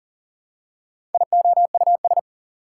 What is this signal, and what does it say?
Morse code, heard as a keyed tone, about 40 wpm, 700 Hz. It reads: IOUS
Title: Prefixes and Suffixes 40wpm